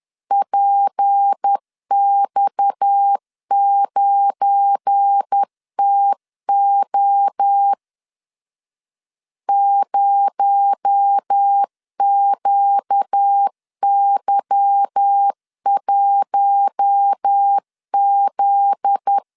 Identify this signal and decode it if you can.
Morse code, heard as a keyed tone, about 10 wpm, 800 Hz. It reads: PX9TO 0QY1Z
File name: LEZIONI / 8 YJ45 / 8 Y 10wpm 2615